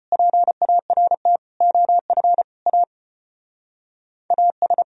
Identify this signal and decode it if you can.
Morse code, heard as a keyed tone, about 31 wpm, 700 Hz. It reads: PART OF A AS